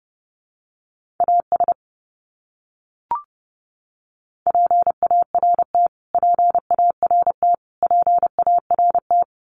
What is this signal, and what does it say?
Morse code, heard as a keyed tone, about 30 wpm, 700 Hz. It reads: AS PART PART PART
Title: Top 200 Words Repeated 30wpm